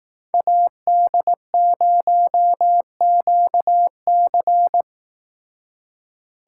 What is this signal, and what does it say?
Morse code, heard as a keyed tone, about 17 wpm, 700 Hz. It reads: AD0QC
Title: Speed Racing - USA 2x2 Call Signs 15wpm